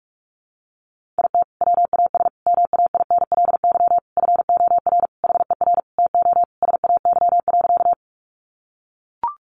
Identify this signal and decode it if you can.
Morse code, heard as a keyed tone, about 44 wpm, 700 Hz. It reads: IT WAS MAINLY FOR HER TO SAY.
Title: Sentences from Top 1400 Words 45wpm